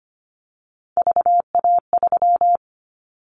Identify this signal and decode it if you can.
Morse code, heard as a keyed tone, about 25 wpm, 700 Hz. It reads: VA3
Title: Callsign Prefixes for Top 50 DXCC - 25wpm